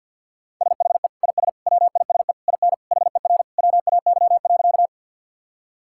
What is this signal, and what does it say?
Morse code, heard as a keyed tone, about 50 wpm, 700 Hz. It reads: SHE IS WISE IN HER WAY.
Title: Sentences from Top 1600 Words 50wpm